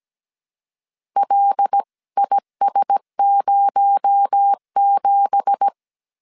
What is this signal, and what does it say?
Morse code, heard as a keyed tone, about 17 wpm, 800 Hz. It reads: LIS07
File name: CW / DET / 1 9 1500C S A 16wpm 2828